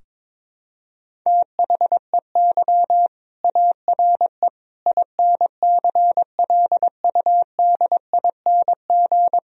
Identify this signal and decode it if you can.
Morse code, heard as a keyed tone, about 21 wpm, 700 Hz. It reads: THEY ARE INCLUDING
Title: Sentences from Top 1000 Words - Review Part 4 - 22wpm